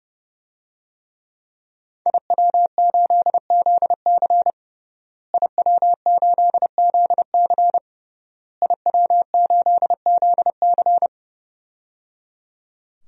Morse code, heard as a keyed tone, about 30 wpm, 700 Hz. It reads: IW8ZC IW8ZC IW8ZC
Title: DXCC-100 Call Signs - 5 Characters - Repeated 3x - 30wpm